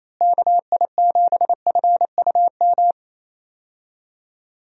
Morse code, heard as a keyed tone, about 28 wpm, 700 Hz. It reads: KI7FUM
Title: USA Call Signs - 6 Characters - Repeated 2x - 28wpm